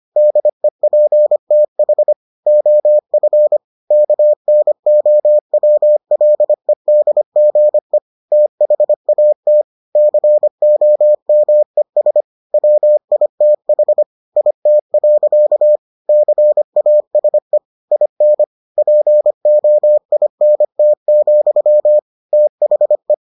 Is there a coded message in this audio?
Yes: Morse code from 0.2 to 23.1 s, heard as a keyed tone, about 25 wpm, 600 Hz. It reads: DEPTH OF KNOWLEDGE THAT COMES WITH IT. CASE IN POINT, THE